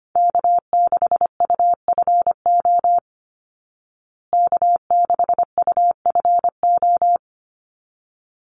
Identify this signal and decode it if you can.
Morse code, heard as a keyed tone, about 25 wpm, 700 Hz. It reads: K6UFO K6UFO
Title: Call Signs Repeated 3x - 25wpm